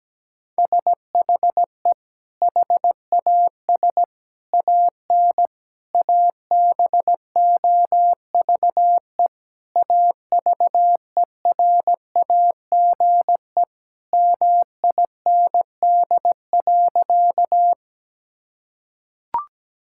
Morse code, heard as a keyed tone, about 17 wpm, 700 Hz. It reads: SHE HAS AN ABOVE AVERAGE MIND.
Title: Sentences from Top 1000 Words - Review Part 6 - 17wpm